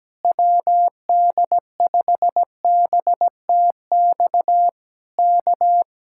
Morse code, heard as a keyed tone, about 17 wpm, 700 Hz. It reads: WD5BTX K
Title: Speed-Racing Common QSO Phrases - 17wpm